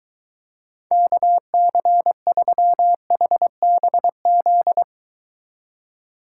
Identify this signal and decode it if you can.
Morse code, heard as a keyed tone, about 23 wpm, 700 Hz. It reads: KC3HBZ